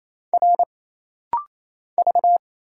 Morse code, heard as a keyed tone, about 29 wpm, 700 Hz. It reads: R V
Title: Single Letters - Rapid-Fire 28wpm